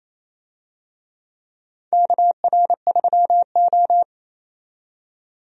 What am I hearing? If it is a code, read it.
Morse code, heard as a keyed tone, about 28 wpm, 700 Hz. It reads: KR3O
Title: USA Call Signs - 4 Characters - 28wpm